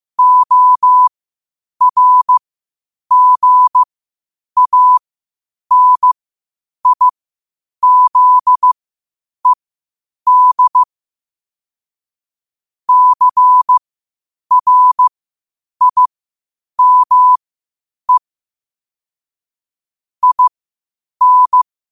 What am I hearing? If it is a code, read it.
Morse code, heard as a keyed tone, about 15 wpm, spaced as for 8 wpm, 1000 Hz. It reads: ORGANIZED CRIME IN
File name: qotdfiles / 8wpm